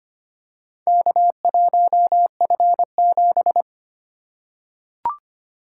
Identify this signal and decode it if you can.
Morse code, heard as a keyed tone, about 25 wpm, 700 Hz. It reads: K1F7
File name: practice / 25wpm / Four Letter-Number Combinations - 25wpm